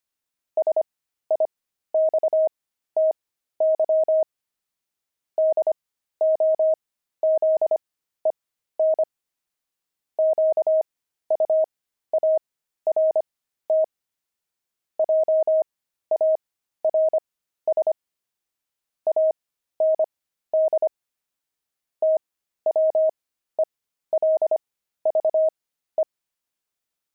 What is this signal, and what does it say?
Morse code, heard as a keyed tone, about 25 wpm, spaced as for 13 wpm, 630 Hz. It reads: SIXTY DOZEN QUART JARS AND TWELVE
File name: pangram-fs-13-cs-25-twelve